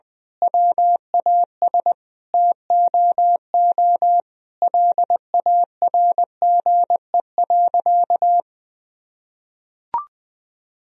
Morse code, heard as a keyed tone, about 21 wpm, 700 Hz. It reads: WAS TOO LARGE.
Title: Sentences from Top 1000 Words - Review Part 1 - 20wpm